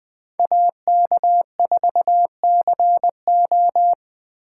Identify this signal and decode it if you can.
Morse code, heard as a keyed tone, about 20 wpm, 700 Hz. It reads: AK4CO
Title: Speed Racing - USA 2x2 Call Signs 17wpm